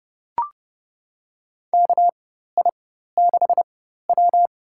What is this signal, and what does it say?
Morse code, heard as a keyed tone, about 32 wpm, spaced as for 14 wpm, 700 Hz. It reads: KI6W
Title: USA Call Signs - 4x Character Spacing - 30wpm